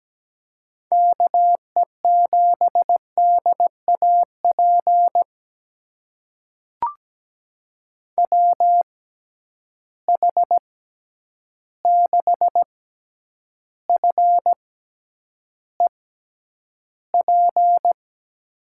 Morse code, heard as a keyed tone, about 17 wpm, spaced as for 9 wpm, 700 Hz. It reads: KE7DAP WH6FEP
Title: USA Call Signs - 6 Characters - 6x Character Spacing - 17wpm